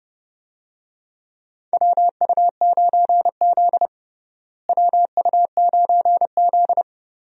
Morse code, heard as a keyed tone, about 30 wpm, 700 Hz. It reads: WU9Z WU9Z
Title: USA Call Signs - 4 Characters - Repeated 3x - 30wpm